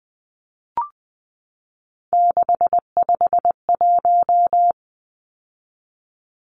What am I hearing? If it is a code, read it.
Morse code, heard as a keyed tone, about 20 wpm, 700 Hz. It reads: 651
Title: Numbers - 3-Digit 20wpm